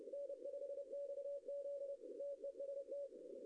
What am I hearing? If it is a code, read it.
Morse code, heard as a keyed tone, about 31 wpm, 560 Hz. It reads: N5XZ TEST
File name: contests / arrl-160-k5tr-op-w5tn -0900